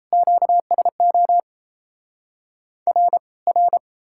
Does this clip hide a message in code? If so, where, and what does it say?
Yes: Morse code from 0.1 to 3.8 s, heard as a keyed tone, about 38 wpm, 700 Hz. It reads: QSO R R